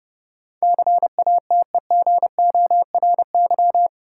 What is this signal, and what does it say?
Morse code, heard as a keyed tone, about 31 wpm, 700 Hz. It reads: CATEGORY
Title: Sentences from Top 1000 Words - Review Part 7 - New Word Drill - 30wpm